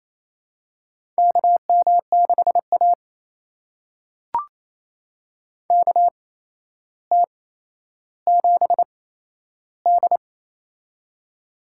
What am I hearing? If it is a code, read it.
Morse code, heard as a keyed tone, about 28 wpm, spaced as for 12 wpm, 700 Hz. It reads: KM6A KT7D
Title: USA Call Signs - 4 Characters - 8x Character Spacing - 28wpm